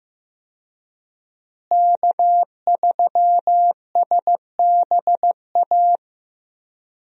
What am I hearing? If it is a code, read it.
Morse code, heard as a keyed tone, about 15 wpm, 700 Hz. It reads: K3SBA
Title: USA Call Signs - 5 Characters - Repeated 2x - 15wpm